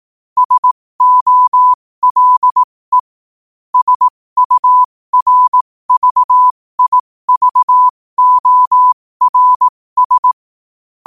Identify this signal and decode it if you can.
Morse code, heard as a keyed tone, about 18 wpm, spaced as for 15 wpm, 1000 Hz. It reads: SOLE SURVIVORS
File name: qotdfiles / 15wpm